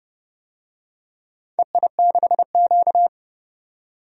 Morse code, heard as a keyed tone, about 30 wpm, 700 Hz. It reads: EI6Q
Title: DXCC-100 Call Signs - 4 Characters - 30wpm